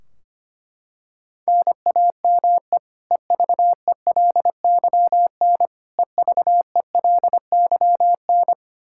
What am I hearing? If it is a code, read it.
Morse code, heard as a keyed tone, about 25 wpm, 700 Hz. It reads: NAME EVELYN EVELYN
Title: Practice QSO - Exchange 1 - Beginning - 25wpm